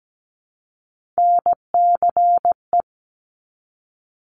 Morse code, heard as a keyed tone, about 17 wpm, 700 Hz. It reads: NCE
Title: Top Three Letter Combinations in Words 17wpm